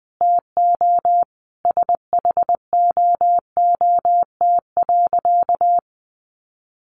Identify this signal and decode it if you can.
Morse code, heard as a keyed tone, about 20 wpm, 700 Hz. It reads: TO SHOOT.
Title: Sentences from Top 1200 Words 20wpm